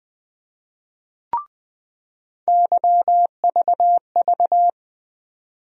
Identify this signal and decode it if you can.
Morse code, heard as a keyed tone, about 21 wpm, 700 Hz. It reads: YVV
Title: USA Call Sign Postfixes - 3 Letters 20wpm